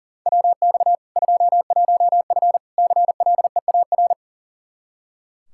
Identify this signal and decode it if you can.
Morse code, heard as a keyed tone, about 40 wpm, 700 Hz. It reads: WX 21F CLEAR
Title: K1USN SST QSO Practice 40wpm